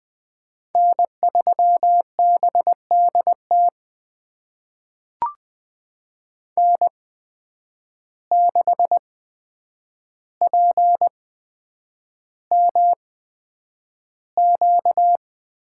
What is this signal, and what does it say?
Morse code, heard as a keyed tone, about 20 wpm, spaced as for 9 wpm, 700 Hz. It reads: N3BDT N6PMQ
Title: USA Call Signs - 5 Characters - 8x Character Spacing - 20wpm